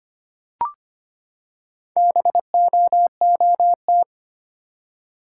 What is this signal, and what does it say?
Morse code, heard as a keyed tone, about 25 wpm, 700 Hz. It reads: BOOT
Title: Four Letter Words - 25wpm